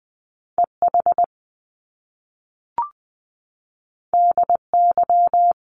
Morse code, heard as a keyed tone, about 20 wpm, 700 Hz. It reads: E H D Y
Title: Top Two Letter Combinations in Words 20wpm